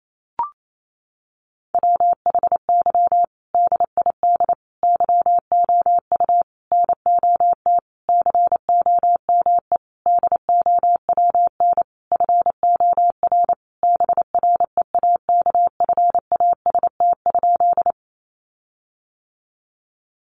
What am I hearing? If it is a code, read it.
Morse code, heard as a keyed tone, about 28 wpm, 700 Hz. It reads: WHY DID YOU NOT COME DOWN FOR BREAKFAST?
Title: Sentences from Top 1500 Words 28wpm